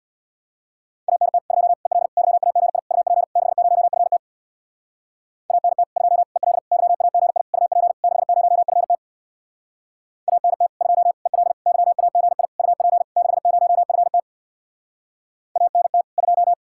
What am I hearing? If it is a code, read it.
Morse code, heard as a keyed tone, about 64 wpm, 700 Hz. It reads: ANT 2 EL YAGI UP 60FT ANT 2 EL YAGI UP 60FT ANT 2 EL YAGI UP 60FT ANT 2